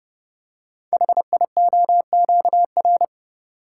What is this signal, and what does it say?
Morse code, heard as a keyed tone, about 30 wpm, 700 Hz. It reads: HIOQR
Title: Code Groups - 5 Letter:Numbers - 30wpm